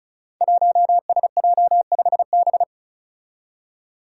Morse code, heard as a keyed tone, about 36 wpm, 700 Hz. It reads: 1SJ5B
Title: Five Challenging Letter-Number Combinations 35wpm